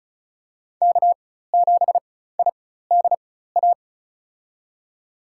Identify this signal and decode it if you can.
Morse code, heard as a keyed tone, about 34 wpm, spaced as for 16 wpm, 700 Hz. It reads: K7IDA
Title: USA Call Signs - 5 Characters - 4x Character Spacing - 35wpm